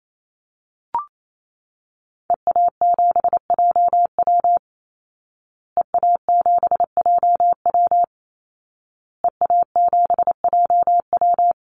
Morse code, heard as a keyed tone, about 28 wpm, 700 Hz. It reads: EA7JW EA7JW EA7JW
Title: Call Signs Repeated 2x - 28wpm